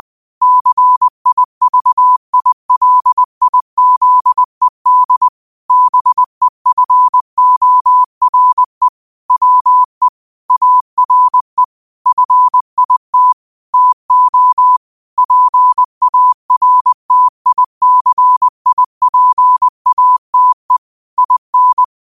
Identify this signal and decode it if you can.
Morse code, heard as a keyed tone, about 20 wpm, 1000 Hz. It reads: CIVILIZED BEFORE WE ARE FIT TO PARTICIPATE IN
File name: qotdfiles / 20wpm